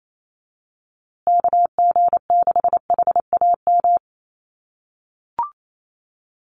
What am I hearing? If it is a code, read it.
Morse code, heard as a keyed tone, about 28 wpm, 700 Hz. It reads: KG6HAM
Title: USA Call Signs Repeated 2x 28wpm